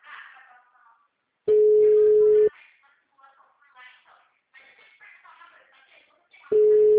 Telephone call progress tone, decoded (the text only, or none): ringback tone